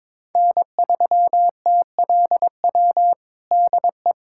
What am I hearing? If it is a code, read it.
Morse code, heard as a keyed tone, about 22 wpm, 700 Hz. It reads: N3TLW DE
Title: USA Callsign DE Callsign - 5 Characters - 22wpm